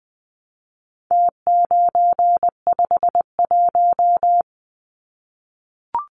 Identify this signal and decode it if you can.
Morse code, heard as a keyed tone, about 20 wpm, 700 Hz. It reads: T951
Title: Numbers - Contest Style - 4-Digit 20wpm